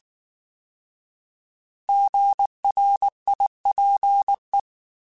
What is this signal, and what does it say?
Morse code, heard as a keyed tone, about 19 wpm, 800 Hz. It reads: GRIPE